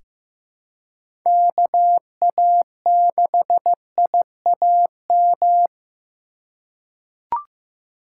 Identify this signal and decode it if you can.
Morse code, heard as a keyed tone, about 15 wpm, 700 Hz. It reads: KA6IAM
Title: USA Call Signs - 6 Characters - 4x Character Spacing - 15wpm